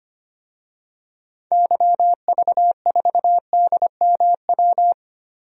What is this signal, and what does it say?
Morse code, heard as a keyed tone, about 25 wpm, 700 Hz. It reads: YV4DMW
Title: DXCC-100 Call Signs - 6 Characters - Repeated 2x - 25wpm